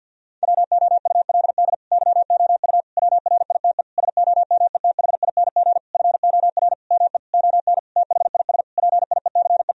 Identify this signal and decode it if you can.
Morse code, heard as a keyed tone, about 50 wpm, 700 Hz. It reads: WOULD YOU WRITE SOMETHING FOR ME ON THIS PIECE
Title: Sentences from Top 1000 Words - Review Part 2 - 50wpm